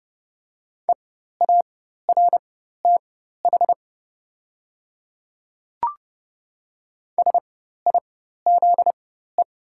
Morse code, heard as a keyed tone, about 30 wpm, spaced as for 14 wpm, 700 Hz. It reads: EARTH SIZE
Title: 4x Character Spacing - Top 500 Words - 30wpm